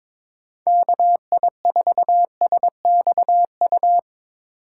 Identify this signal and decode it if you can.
Morse code, heard as a keyed tone, about 22 wpm, 700 Hz. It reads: KI4SXU
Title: USA Call Signs - 6 Characters - 6x Character Spacing - 22wpm